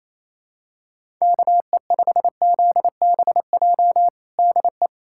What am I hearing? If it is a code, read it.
Morse code, heard as a keyed tone, about 28 wpm, 700 Hz. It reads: KE5ZBJ DE
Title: USA Callsign DE Callsign 28wpm